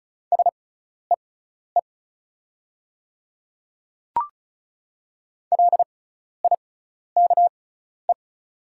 Morse code, heard as a keyed tone, about 36 wpm, spaced as for 12 wpm, 690 Hz. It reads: SEE LIKE
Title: 6x Character Spacing - Top 100 Words - 35wpm